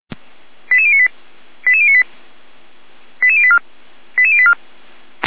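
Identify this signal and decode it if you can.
Five-tone selective call (ZVEI), heard as quick runs of five tones, tones of about 70 ms. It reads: 88098 88098 88084 88084
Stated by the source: Received on SDR as a ZVEI five-tone selective call, code 88571.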